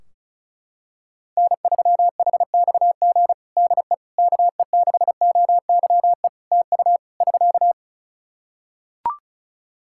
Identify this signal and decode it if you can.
Morse code, heard as a keyed tone, about 34 wpm, 700 Hz. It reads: N3HXG DE KE6OYE TU <SK>
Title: Practice QSO - Exchange 4 - Ending - 35wpm